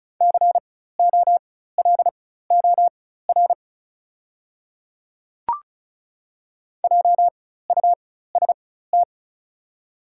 Morse code, heard as a keyed tone, about 34 wpm, spaced as for 16 wpm, 700 Hz. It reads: COLOR JUST
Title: 4x Character Spacing - Top 300 Words - 35wpm